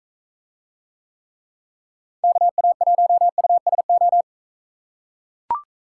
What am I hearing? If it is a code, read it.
Morse code, heard as a keyed tone, about 44 wpm, 700 Hz. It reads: KA1USO